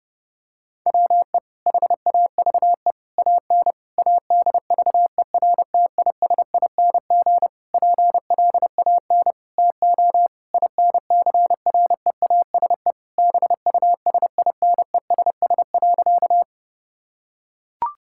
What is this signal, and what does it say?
Morse code, heard as a keyed tone, about 28 wpm, 700 Hz. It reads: WE HAVE AN ADVERTISING PLAN TO INCREASE BUSINESS.
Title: Sentences from Top 1000 Words - Review Part 6 - 30wpm